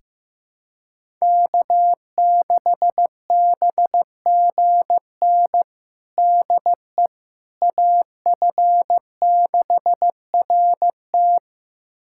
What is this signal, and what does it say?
Morse code, heard as a keyed tone, about 15 wpm, 700 Hz. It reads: K6BGN DE AF6RT
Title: USA Callsign DE Callsign - 5 Characters - 15wpm